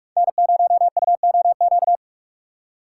Morse code, heard as a keyed tone, about 46 wpm, 700 Hz. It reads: N0UOQ
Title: USA Call Signs - 5 Characters - 2x Character Spacing - 45wpm